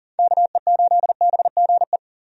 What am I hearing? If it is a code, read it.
Morse code, heard as a keyed tone, about 37 wpm, 700 Hz. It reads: KE8BGE
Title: USA Call Signs - 6 Characters - 40wpm